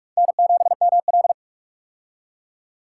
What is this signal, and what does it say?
Morse code, heard as a keyed tone, about 45 wpm, 700 Hz. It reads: N7ML
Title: K1USN SST QSO Practice - 45wpm